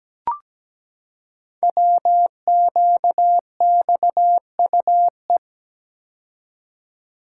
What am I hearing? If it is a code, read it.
Morse code, heard as a keyed tone, about 17 wpm, 700 Hz. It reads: WQXUE
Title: Code Groups - 5 Letters - 17wpm